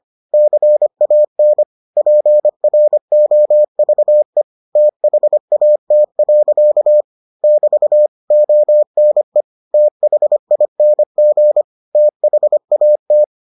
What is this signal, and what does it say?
Morse code, heard as a keyed tone, about 25 wpm, 600 Hz. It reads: CAN PROVE THAT. <BT> ONE THING THAT